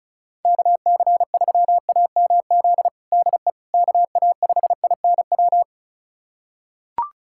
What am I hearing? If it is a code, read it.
Morse code, heard as a keyed tone, about 34 wpm, 700 Hz. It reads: KC3AMZ DE KA5INW
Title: USA Callsign DE Callsign - 6 Characters - 35wpm